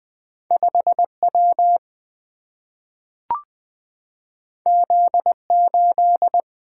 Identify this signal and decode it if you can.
Morse code, heard as a keyed tone, about 20 wpm, 700 Hz. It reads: 5W Z8
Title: Single-Letter Single-Number 20wpm